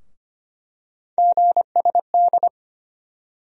Morse code, heard as a keyed tone, about 25 wpm, 700 Hz. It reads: GSD
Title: USA Call Sign Postfixes - 3 Letters 25wpm